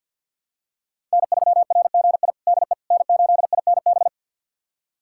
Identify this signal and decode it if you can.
Morse code, heard as a keyed tone, about 53 wpm, 700 Hz. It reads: N3RGI DE N8INB
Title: USA Callsign DE Callsign - 5 Characters - 50wpm